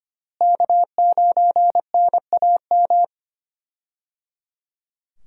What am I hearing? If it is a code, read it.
Morse code, heard as a keyed tone, about 25 wpm, 700 Hz. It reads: K9NAM
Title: USA Call Signs - 5 Characters - Repeated 3x - 25wpm